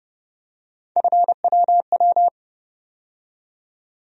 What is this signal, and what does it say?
Morse code, heard as a keyed tone, about 30 wpm, 700 Hz. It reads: FWW
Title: USA Call Sign Postfixes - 3 Letters 30wpm